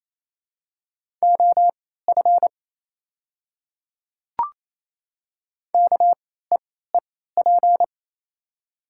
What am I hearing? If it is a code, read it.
Morse code, heard as a keyed tone, about 28 wpm, spaced as for 16 wpm, 700 Hz. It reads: OF KEEP
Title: 3x Character Spacing - Top 300 Words - 28wpm